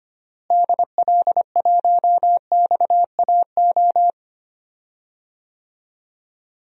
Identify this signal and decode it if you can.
Morse code, heard as a keyed tone, about 25 wpm, 700 Hz. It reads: DL1XAO
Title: DXCC-100 Call Signs - 6 Characters - Repeated 2x - 25wpm